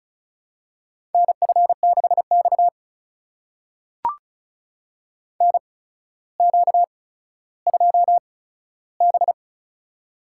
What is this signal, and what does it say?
Morse code, heard as a keyed tone, about 36 wpm, spaced as for 15 wpm, 700 Hz. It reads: NF6X NQ2B
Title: USA Call Signs - 4 Characters - 8x Character Spacing - 35wpm